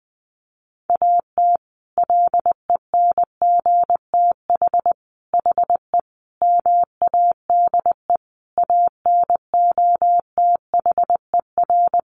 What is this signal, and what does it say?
Morse code, heard as a keyed tone, about 20 wpm, 700 Hz. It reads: AT LENGTH HE MADE ANOTHER
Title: Sentences from Top 1000 Words 20wpm